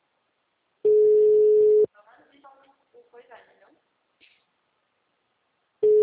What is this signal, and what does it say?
Ringback tone (Europe, 425 Hz), with speech.